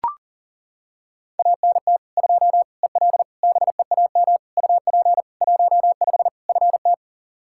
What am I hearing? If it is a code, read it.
Morse code, heard as a keyed tone, about 40 wpm, 700 Hz. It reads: ANT 2 EL BEAM UP 15 FT
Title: Practice QSO - Exchange 2 - Info - 40wpm